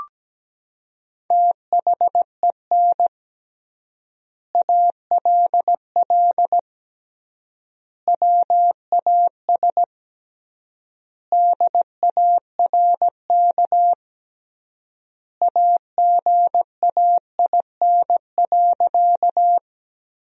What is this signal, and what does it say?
Morse code, heard as a keyed tone, about 17 wpm, 700 Hz. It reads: THEN ALL WAS DARK AGAIN.
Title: 3x Word-Spacing - Sentences from Top 500 Words - 17wpm